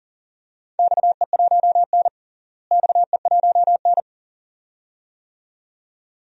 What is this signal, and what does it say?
Morse code, heard as a keyed tone, about 40 wpm, 700 Hz. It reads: XE1N XE1N
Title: DXCC-100 Call Signs - 4 Characters - Repeated 2x - 40wpm